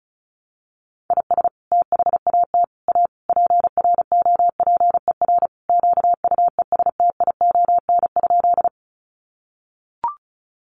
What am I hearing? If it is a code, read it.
Morse code, heard as a keyed tone, about 34 wpm, 700 Hz. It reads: IS THAT A PROPER QUESTION?
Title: Sentences from Top 1100 Words 35wpm